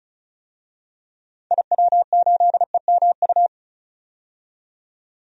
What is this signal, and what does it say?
Morse code, heard as a keyed tone, about 36 wpm, 700 Hz. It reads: IW8EMU
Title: DXCC-100 Call Signs - 6 Characters - 35wpm